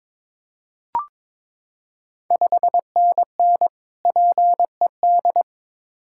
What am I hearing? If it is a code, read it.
Morse code, heard as a keyed tone, about 21 wpm, 690 Hz. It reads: 5NN PED
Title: Ontario QSO Party Exchange 22wpm